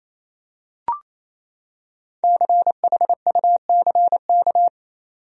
Morse code, heard as a keyed tone, about 28 wpm, 700 Hz. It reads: CHUCK
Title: USA Common Names - Max 5 Letters - 28wpm